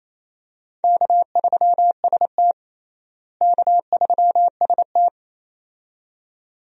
Morse code, heard as a keyed tone, about 28 wpm, 700 Hz. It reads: K3ST K3ST
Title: USA Call Signs - 4 Characters - Repeated 2x - 28wpm